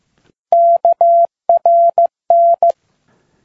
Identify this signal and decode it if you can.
Morse code, heard as a keyed tone, about 15 wpm, 680 Hz. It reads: KRN